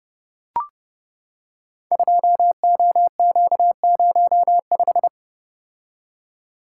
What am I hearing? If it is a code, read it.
Morse code, heard as a keyed tone, about 30 wpm, 700 Hz. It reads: 2OQ05
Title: Code Groups - 5 Letter:Numbers - 30wpm